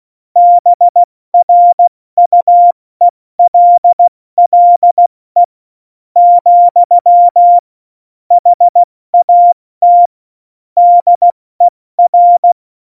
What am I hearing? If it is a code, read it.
Morse code, heard as a keyed tone, about 16 wpm, spaced as for 13 wpm, 700 Hz. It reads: BRUELLE, HAT DER
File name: Morsekurs / Witze / Witz 18. Dezember, 70 F80